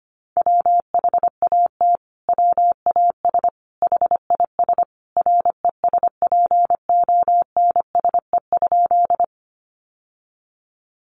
Morse code, heard as a keyed tone, about 25 wpm, 700 Hz. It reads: WHAT WAS HIS RESPONSE?
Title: Sentences from Top 1400 Words 25wpm